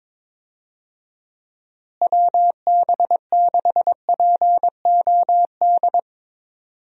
Morse code, heard as a keyed tone, about 21 wpm, 700 Hz. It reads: WB6POD
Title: USA Call Signs - 6 Characters - Repeated 3x - 22wpm